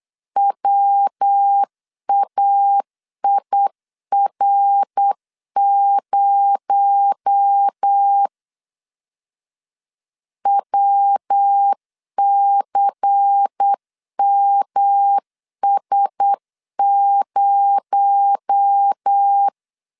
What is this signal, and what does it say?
Morse code, heard as a keyed tone, about 8 wpm, 800 Hz. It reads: WAIR0 WCMS0
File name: CW / DET / 1 6 1500C S C 8wpm 5219